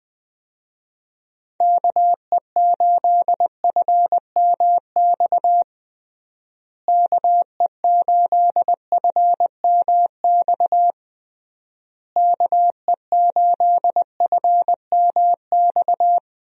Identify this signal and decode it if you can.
Morse code, heard as a keyed tone, about 20 wpm, 700 Hz. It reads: KE8FMX KE8FMX KE8FMX
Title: USA Call Signs - 6 Characters - Repeated 3x - 20wpm